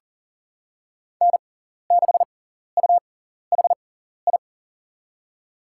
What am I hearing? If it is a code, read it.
Morse code, heard as a keyed tone, about 40 wpm, spaced as for 14 wpm, 700 Hz. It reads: N6UHI
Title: USA Call Signs - 5 Characters - 6x Character Spacing - 40wpm